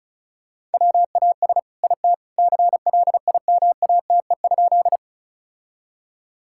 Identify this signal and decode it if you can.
Morse code, heard as a keyed tone, about 34 wpm, 700 Hz. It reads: WAS IT CLIMATE?
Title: Sentences from Top 1600 Words 35wpm